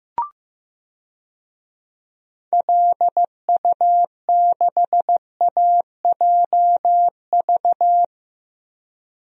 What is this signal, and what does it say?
Morse code, heard as a keyed tone, about 16 wpm, 700 Hz. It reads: LU6AJV
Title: DXCC-100 Call Signs - 6 Characters - Repeated 2x - 15wpm